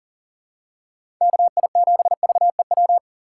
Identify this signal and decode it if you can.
Morse code, heard as a keyed tone, about 40 wpm, 700 Hz. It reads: KI7VEW
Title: USA Call Signs - 6 Characters - 6x Character Spacing - 40wpm